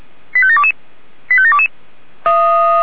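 Five-tone selective call (ZVEI), heard as quick runs of five tones, tones of about 70 ms. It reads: 85722 85722
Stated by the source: ZVEI five-tone selective call received on SDR.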